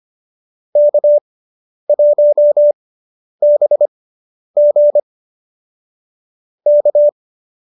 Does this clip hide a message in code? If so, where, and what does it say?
Yes: Morse code from 0.8 to 7.1 s, heard as a keyed tone, about 25 wpm, spaced as for 10 wpm, 590 Hz. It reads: K1BG K